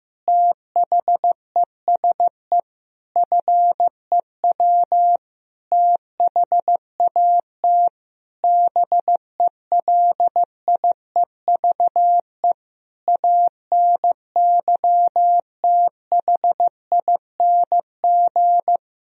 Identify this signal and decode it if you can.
Morse code, heard as a keyed tone, about 15 wpm, 700 Hz. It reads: THESE FEW THAT BELIEVE ANYTHING
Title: Sentences from Top 1000 Words - Review Part 1 - 15wpm